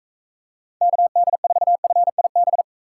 Morse code, heard as a keyed tone, about 44 wpm, 700 Hz. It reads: KD4FIB